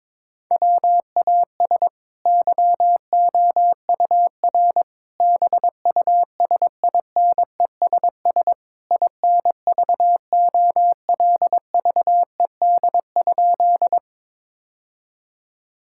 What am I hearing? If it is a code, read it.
Morse code, heard as a keyed tone, about 22 wpm, 700 Hz. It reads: WAS YOUR BUSINESS INVOLVED?
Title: Sentences from Top 1000 Words - Review Part 5 - 22wpm